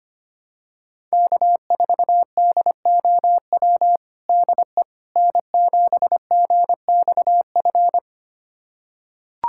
Morse code, heard as a keyed tone, about 25 wpm, 700 Hz. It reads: K4DOW DE N7GXF
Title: USA Callsign DE Callsign - 5 Characters - 25wpm